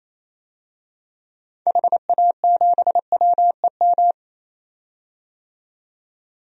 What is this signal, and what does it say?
Morse code, heard as a keyed tone, about 28 wpm, 700 Hz. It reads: HA7WEM
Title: DXCC-100 Call Signs - 6 Characters - 28wpm